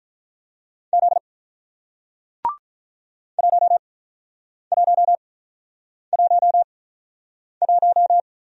Morse code, heard as a keyed tone, about 48 wpm, spaced as for 9 wpm, 700 Hz. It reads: Z 1111